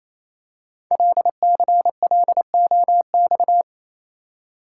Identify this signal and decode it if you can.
Morse code, heard as a keyed tone, about 28 wpm, 700 Hz. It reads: LCLOX
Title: Code Groups - 5 Letter:Numbers - 28wpm